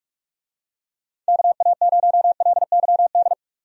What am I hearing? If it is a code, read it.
Morse code, heard as a keyed tone, about 46 wpm, 700 Hz. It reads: KA0LYD
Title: USA Call Signs - 6 Characters - 45wpm